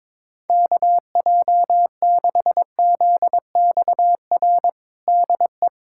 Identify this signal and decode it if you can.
Morse code, heard as a keyed tone, about 22 wpm, 700 Hz. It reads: KJ6ZXR DE
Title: USA Callsign DE Callsign - 6 Characters - 22wpm